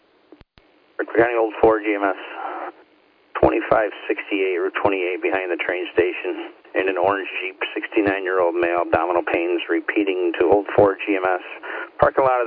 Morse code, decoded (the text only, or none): none